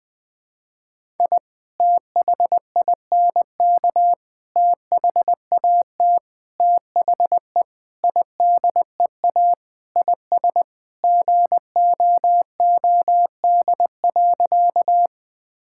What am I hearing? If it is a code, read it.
Morse code, heard as a keyed tone, about 20 wpm, 700 Hz. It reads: I THINK THAT THE IDEA IS GOOD.
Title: Sentences with 4-Word Phrases - 20wpm